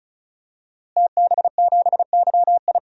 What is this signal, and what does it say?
Morse code, heard as a keyed tone, about 34 wpm, 700 Hz. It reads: TB7YI
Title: DXCC-100 Call Signs - 5 Characters - Repeated 2x - 35wpm